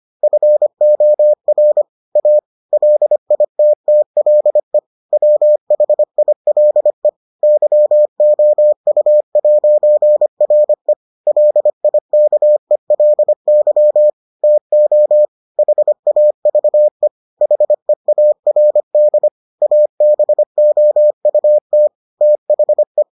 Morse code, heard as a keyed tone, about 25 wpm, 600 Hz. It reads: FOR A LITTLE WHILE YOU'RE LIKELY TO HAVE HEARD ABOUT THE